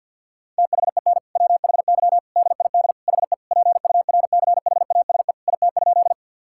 Morse code, heard as a keyed tone, about 50 wpm, 700 Hz. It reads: THEN WHY DID HE PURCHASE IT?
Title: Sentences from Top 1000 Words - Review Part 5 - 50wpm